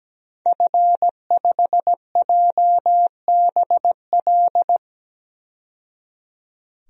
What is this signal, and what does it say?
Morse code, heard as a keyed tone, about 17 wpm, 700 Hz. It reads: F5JBL
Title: DXCC-100 Call Signs - 5 Characters - 17wpm